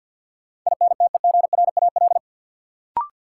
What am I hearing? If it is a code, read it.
Morse code, heard as a keyed tone, about 53 wpm, 700 Hz. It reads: INTEGRAL
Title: Sets of 5 Words - New Word Drill - 50wpm